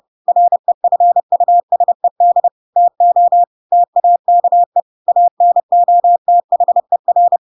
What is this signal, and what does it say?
Morse code, heard as a keyed tone, about 30 wpm, 700 Hz. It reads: REFUSED TO TAKE ANOTHER
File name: Pinocchio / Pinocchio ch3 30 30 ik1hge